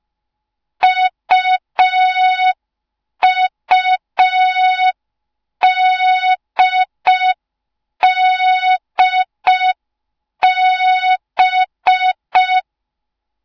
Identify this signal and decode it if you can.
Morse code, heard as a keyed tone, about 5 wpm, 3700 Hz. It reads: UUDDB
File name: Morze-HA2MN / hang / 06BDU HA2MN